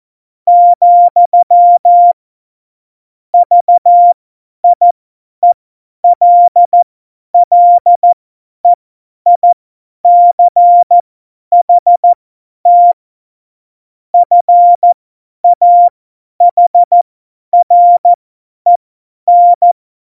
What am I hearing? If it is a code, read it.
Morse code, heard as a keyed tone, about 14 wpm, spaced as for 10 wpm, 700 Hz. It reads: , VIELLEICHT FAHREN
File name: Morsekurs / Witze / Witz 08. Januar 2019, 50 F70